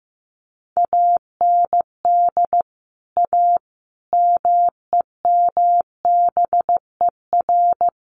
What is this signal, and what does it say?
Morse code, heard as a keyed tone, about 15 wpm, 700 Hz. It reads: AND A MEMBER
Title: Sets of 5 Words 15wpm